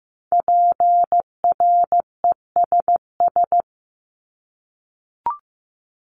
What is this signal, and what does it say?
Morse code, heard as a keyed tone, about 15 wpm, 700 Hz. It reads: PRESS
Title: Top 500 Words 15wpm